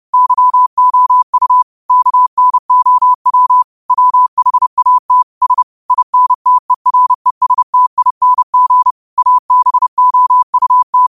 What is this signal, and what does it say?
Morse code, heard as a keyed tone, about 30 wpm, 1000 Hz. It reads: YOU KNOW WHAT S INTERESTING ABOUT
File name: qotdfiles / 30wpm